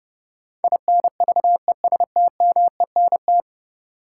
Morse code, heard as a keyed tone, about 31 wpm, 700 Hz. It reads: INVESTMENT
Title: Sentences from Top 700 Words - New Word Drill - 30wpm